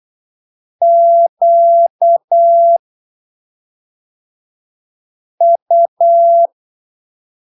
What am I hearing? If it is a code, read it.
Morse code, heard as a keyed tone, about 8 wpm, spaced as for 3 wpm, 680 Hz. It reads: QU
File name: arquivos / texto / texto 17 08ppm